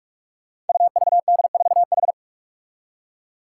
Morse code, heard as a keyed tone, about 44 wpm, 700 Hz. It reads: UVD4H
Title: Five Challenging Letter-Number Combinations 45wpm